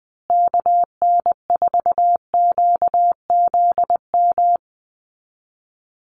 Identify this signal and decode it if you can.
Morse code, heard as a keyed tone, about 20 wpm, 700 Hz. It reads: KN4QZM
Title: USA Call Signs Repeated 3x 20wpm